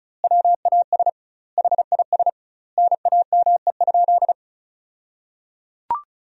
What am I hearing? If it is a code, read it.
Morse code, heard as a keyed tone, about 36 wpm, 700 Hz. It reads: WAS HIS NAME?
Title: 2x Word-Spacing - Sentences from Top 200 Words - 35wpm